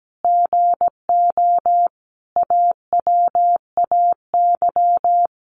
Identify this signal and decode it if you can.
Morse code, heard as a keyed tone, about 17 wpm, 700 Hz. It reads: GO AWAY
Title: Sentences from Top 1100 Words 17wpm